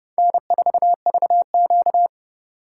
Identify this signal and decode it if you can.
Morse code, heard as a keyed tone, about 29 wpm, 700 Hz. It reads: N4VQ